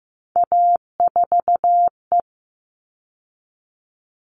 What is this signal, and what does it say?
Morse code, heard as a keyed tone, about 15 wpm, 700 Hz. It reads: A4E
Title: All Cut Numbers - 3-Digit 15wpm